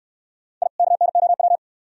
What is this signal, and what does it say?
Morse code, heard as a keyed tone, about 69 wpm, 700 Hz. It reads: I KNOW